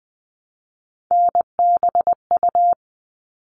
Morse code, heard as a keyed tone, about 19 wpm, 700 Hz. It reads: NBU
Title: All Cut Numbers - 3-Digit 20wpm